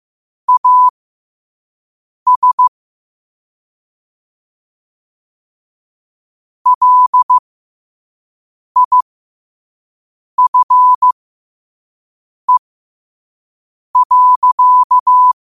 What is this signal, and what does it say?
Morse code, heard as a keyed tone, about 15 wpm, spaced as for 5 wpm, 1000 Hz. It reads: AS LIFE.
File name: qotdfiles / 5wpm